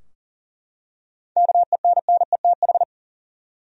Morse code, heard as a keyed tone, about 40 wpm, 700 Hz. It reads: KENNETH
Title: USA Common Names - 40wpm